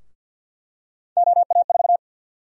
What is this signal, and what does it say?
Morse code, heard as a keyed tone, about 51 wpm, 700 Hz. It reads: OA4
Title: USA Call Sign Prefixes - Repeated 2x - 50wpm